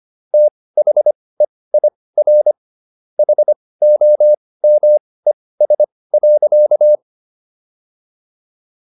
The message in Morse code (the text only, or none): THEIR HOMES.